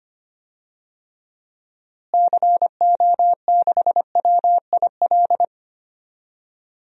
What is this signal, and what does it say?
Morse code, heard as a keyed tone, about 25 wpm, 700 Hz. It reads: CO6WIL
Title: DXCC-100 Call Signs - 6 Characters - Repeated 2x - 25wpm